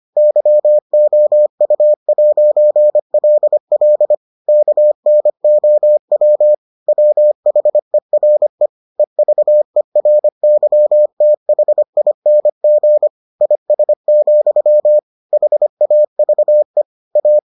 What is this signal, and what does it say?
Morse code, heard as a keyed tone, about 25 wpm, 600 Hz. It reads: YOU'LL KNOW WHERE EVERYTHING IS, HAVE A